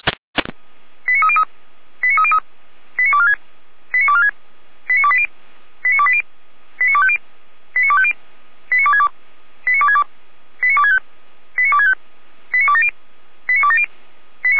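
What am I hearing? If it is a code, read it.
Five-tone selective call (ZVEI), heard as quick runs of five tones, tones of about 70 ms. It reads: 89303 89303 89246 89246 89280 89280 89250 89250 89262 89262 89265 89265 89270 89270
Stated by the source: ZVEI five-tone selective call received with an SDR, logged on the Feuerwehr Orlamünde audio server.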